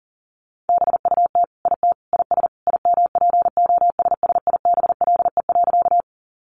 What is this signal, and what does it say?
Morse code, heard as a keyed tone, about 40 wpm, 700 Hz. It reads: BUT IT IS IMPOSSIBLE.
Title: Sentences from Top 1100 Words 40wpm